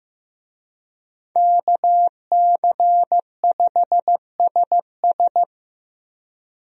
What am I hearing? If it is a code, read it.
Morse code, heard as a keyed tone, about 15 wpm, 700 Hz. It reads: KC5SS
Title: USA Call Signs - 5 Characters - Repeated 2x - 15wpm